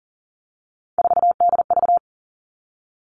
Morse code, heard as a keyed tone, about 37 wpm, 700 Hz. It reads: 4DV
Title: Three Challenging Letter-Number Combinations 40wpm